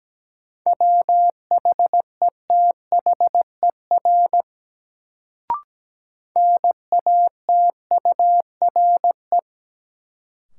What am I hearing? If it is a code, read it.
Morse code, heard as a keyed tone, about 17 wpm, 700 Hz. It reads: WHETHER NATURE
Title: Sets of 3 Words - New Word Drill - 17wpm